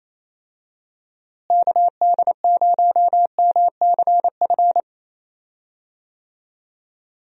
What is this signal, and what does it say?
Morse code, heard as a keyed tone, about 28 wpm, 700 Hz. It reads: KD0MCF
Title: USA Call Signs - 6 Characters - Repeated 2x - 28wpm